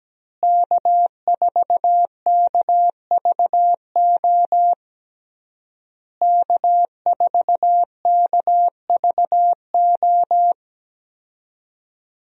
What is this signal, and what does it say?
Morse code, heard as a keyed tone, about 17 wpm, 700 Hz. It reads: K4KVO K4KVO
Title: USA Call Signs - 5 Characters - Repeated 3x - 17wpm